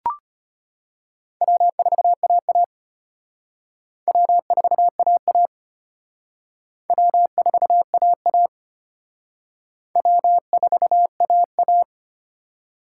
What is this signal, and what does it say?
Morse code, heard as a keyed tone, about 31 wpm, 700 Hz. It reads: W4AA W4AA W4AA W4AA
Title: Speed Racing - USA 1x2 Call Signs 25wpm